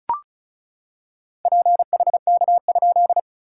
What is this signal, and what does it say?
Morse code, heard as a keyed tone, about 34 wpm, 690 Hz. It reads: PHK?
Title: SOTA QSO Practice 35wpm